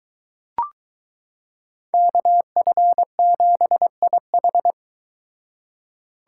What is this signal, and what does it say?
Morse code, heard as a keyed tone, about 23 wpm, 700 Hz. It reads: KF7IH